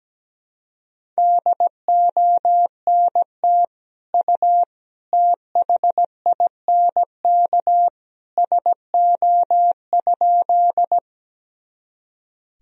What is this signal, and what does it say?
Morse code, heard as a keyed tone, about 17 wpm, 700 Hz. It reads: DONT U THINK SO?
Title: Sentences from Top 300 Words and Abbreviations 17wpm